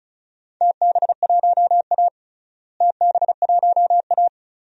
Morse code, heard as a keyed tone, about 36 wpm, 700 Hz. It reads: TB1A TB1A
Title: DXCC-100 Call Signs - 4 Characters - Repeated 3x - 35wpm